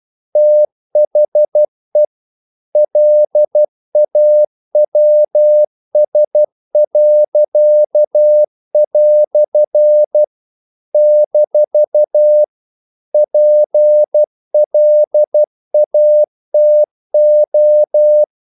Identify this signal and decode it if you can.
Morse code, heard as a keyed tone, about 12 wpm, 600 Hz. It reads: THE LAWS." - PLATO
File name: CW / Practice / EnglishQuotations0112